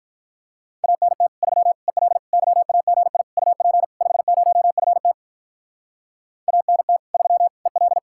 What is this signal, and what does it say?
Morse code, heard as a keyed tone, about 44 wpm, 700 Hz. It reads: ANT 3 EL YAGI UP 50FT ANT 3 EL